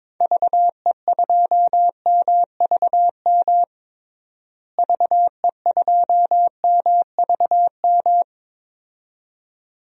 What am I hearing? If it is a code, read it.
Morse code, heard as a keyed tone, about 22 wpm, 700 Hz. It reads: VE2MVM VE2MVM
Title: DXCC-100 Call Signs - 6 Characters - Repeated 2x - 22wpm